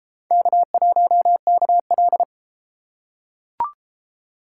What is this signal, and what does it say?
Morse code, heard as a keyed tone, about 31 wpm, 700 Hz. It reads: K1KL